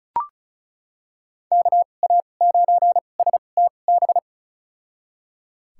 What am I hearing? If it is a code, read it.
Morse code, heard as a keyed tone, about 36 wpm, spaced as for 25 wpm, 700 Hz. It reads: KA9STB
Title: USA Call Signs - 6 Characters - 2x Character Spacing - 35wpm